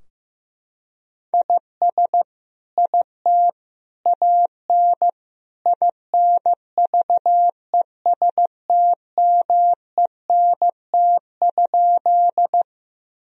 Morse code, heard as a keyed tone, about 15 wpm, 700 Hz. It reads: IS IT AN INVESTMENT?
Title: Sentences from Top 1000 Words - Review Part 4 - 15wpm